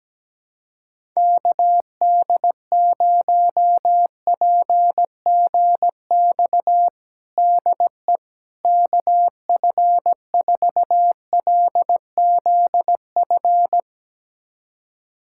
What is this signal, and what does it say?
Morse code, heard as a keyed tone, about 17 wpm, 700 Hz. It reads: KD0PGX DE KF4LZF
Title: USA Callsign DE Callsign 17wpm